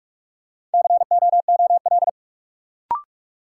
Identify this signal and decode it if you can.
Morse code, heard as a keyed tone, about 45 wpm, 700 Hz. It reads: COOL
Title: Sentences from Top 1200 Words - New Word Drill Alt - 45wpm